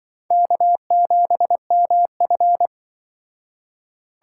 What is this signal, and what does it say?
Morse code, heard as a keyed tone, about 25 wpm, 700 Hz. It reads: K7MF
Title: Speed Racing - USA 1x2 Call Signs 20wpm